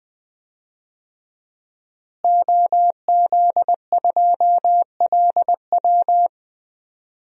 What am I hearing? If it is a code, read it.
Morse code, heard as a keyed tone, about 20 wpm, 700 Hz. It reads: OZ2LW
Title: DXCC-100 Call Signs - 5 Characters - Repeated 3x - 20wpm